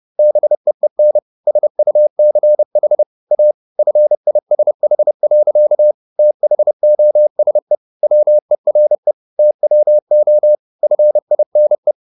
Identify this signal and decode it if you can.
Morse code, heard as a keyed tone, about 30 wpm, 600 Hz. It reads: BEEN SUCH A FISH. THOSE WERE TWO FINE